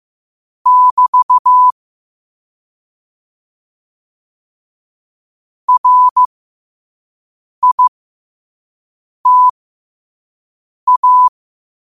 Morse code, heard as a keyed tone, about 15 wpm, spaced as for 5 wpm, 1000 Hz. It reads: <BT> RITA
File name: qotdfiles / 5wpm